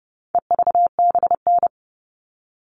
Morse code, heard as a keyed tone, about 30 wpm, 700 Hz. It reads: EVBN
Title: All Cut Numbers - 4-Digit 30wpm